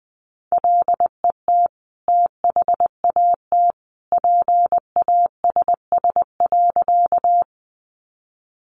Morse code, heard as a keyed tone, about 20 wpm, 700 Hz. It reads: LET THAT PASS.
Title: Sentences from Top 400 Words 20wpm